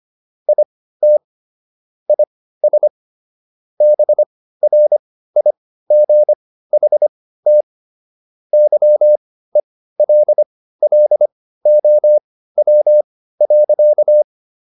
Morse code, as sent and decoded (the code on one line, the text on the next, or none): .. - / .. ... / -... .-. .. --. .... - / -.-- . .-.. .-.. --- .-- .-.-.-
IT IS BRIGHT YELLOW.